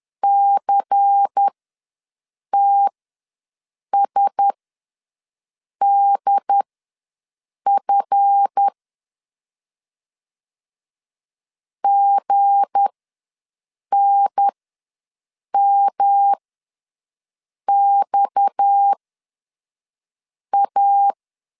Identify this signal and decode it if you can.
Morse code, heard as a keyed tone, about 10 wpm, spaced as for 5 wpm, 800 Hz. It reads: CTSDF GNMXA
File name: LEZIONI / 9 2 ws 10wpm 4130